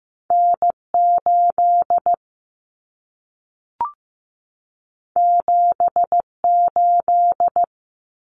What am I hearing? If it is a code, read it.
Morse code, heard as a keyed tone, about 15 wpm, 700 Hz. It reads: N8 78
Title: Common Cut Numbers - 2-Digit 15wpm